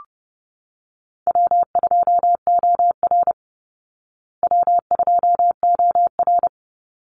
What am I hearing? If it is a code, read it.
Morse code, heard as a keyed tone, about 30 wpm, 700 Hz. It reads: W2OR W2OR
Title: Easy Call Signs Repeated 3x - 30wpm